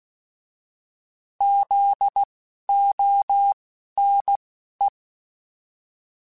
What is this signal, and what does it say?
Morse code, heard as a keyed tone, about 16 wpm, spaced as for 11 wpm, 800 Hz. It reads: ZONE
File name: Alphabetical / Alphabetical mp3s / Zs @10